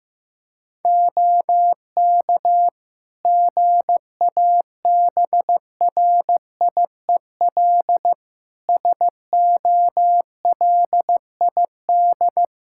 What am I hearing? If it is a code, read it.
Morse code, heard as a keyed tone, about 15 wpm, 700 Hz. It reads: OK GABRIEL SOLID
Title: Practice QSO - Exchange 3 - More Info - 15wpm